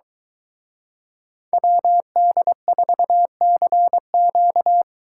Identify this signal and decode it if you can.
Morse code, heard as a keyed tone, about 23 wpm, 700 Hz. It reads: WD4CQ